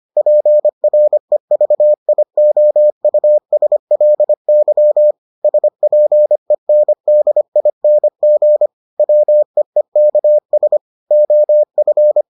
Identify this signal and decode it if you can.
Morse code, heard as a keyed tone, about 25 wpm, 600 Hz. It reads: PREVIOUSLY SPENDING WEEKS OF